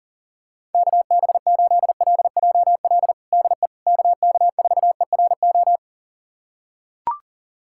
Morse code, heard as a keyed tone, about 37 wpm, 700 Hz. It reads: KB8LJL DE KK4ERO
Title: USA Callsign DE Callsign 40wpm